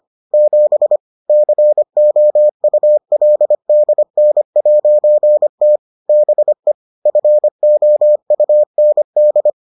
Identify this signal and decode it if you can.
Morse code, heard as a keyed tone, about 25 wpm, 600 Hz. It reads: 7 COULDN'T BE FOUND